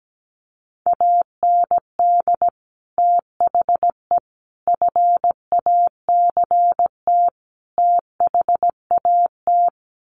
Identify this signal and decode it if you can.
Morse code, heard as a keyed tone, about 17 wpm, 700 Hz. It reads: AND THE FACT THAT
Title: Sets of 4 Words 17wpm